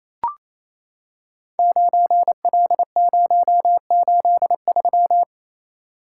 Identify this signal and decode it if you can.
Morse code, heard as a keyed tone, about 27 wpm, 700 Hz. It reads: 9L083
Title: Code Groups - 5 Letter:Numbers - 28wpm